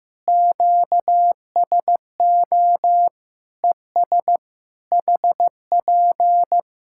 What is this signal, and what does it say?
Morse code, heard as a keyed tone, about 15 wpm, 700 Hz. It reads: QSO ES HP
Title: Practice QSO - Exchange 4 - Ending - 15wpm